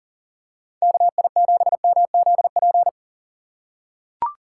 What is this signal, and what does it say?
Morse code, heard as a keyed tone, about 40 wpm, 690 Hz. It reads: KI7MZP
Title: USA Call Signs - 8x Character Spacing - 40wpm